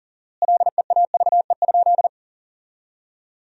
Morse code, heard as a keyed tone, about 40 wpm, 700 Hz. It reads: LEAVE?
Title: 4x Word-Spacing - Sentences from Top 400 Words - 40wpm